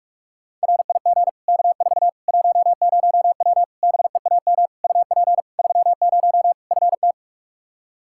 Morse code, heard as a keyed tone, about 43 wpm, 700 Hz. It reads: RIG K4 10W BEAM UP 30 FT
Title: K1USN SST QSO Practice 45wpm